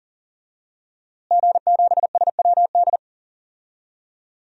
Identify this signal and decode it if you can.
Morse code, heard as a keyed tone, about 39 wpm, 700 Hz. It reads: G7SWD
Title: DXCC-100 Call Signs - 5 Characters - Repeated 2x - 40wpm